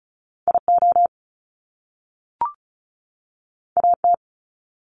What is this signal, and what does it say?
Morse code, heard as a keyed tone, about 36 wpm, 700 Hz. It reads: IO AT
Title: Top Two Letter Combinations in Words 35wpm